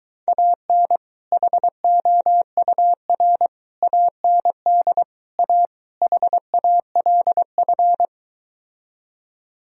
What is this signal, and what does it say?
Morse code, heard as a keyed tone, about 23 wpm, 700 Hz. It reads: AN HOUR AND A HALF